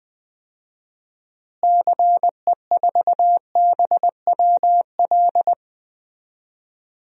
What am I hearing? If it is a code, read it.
Morse code, heard as a keyed tone, about 20 wpm, 700 Hz. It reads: CE4BWL
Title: DXCC-100 Call Signs - 6 Characters - Repeated 2x - 20wpm